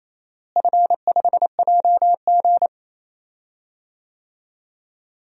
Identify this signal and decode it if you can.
Morse code, heard as a keyed tone, about 28 wpm, 700 Hz. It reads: F5JG
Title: DXCC-100 Call Signs - 4 Characters - Repeated 2x - 28wpm